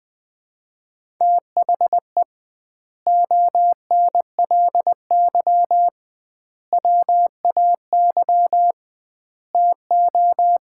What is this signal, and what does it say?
Morse code, heard as a keyed tone, about 20 wpm, 700 Hz. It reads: THE ONLY WAY TO
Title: 2x Word-Spacing - Sets of 4 Words - 20wpm